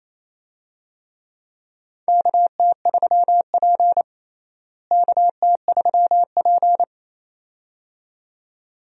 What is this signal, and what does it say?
Morse code, heard as a keyed tone, about 28 wpm, 700 Hz. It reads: KT3P KT3P
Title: USA Call Signs - 4 Characters - Repeated 2x - 28wpm